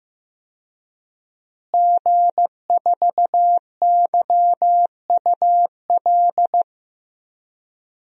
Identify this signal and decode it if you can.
Morse code, heard as a keyed tone, about 15 wpm, 700 Hz. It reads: G4YUL
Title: DXCC-100 Call Signs - 5 Characters - Repeated 3x - 15wpm